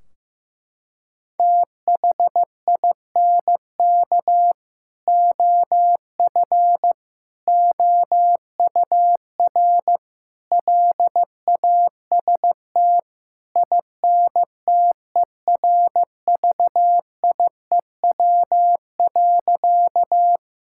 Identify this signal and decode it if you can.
Morse code, heard as a keyed tone, about 15 wpm, 700 Hz. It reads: THINK OF OUR LAST INTERVIEW.
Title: Sentences from Top 1600 Words 15wpm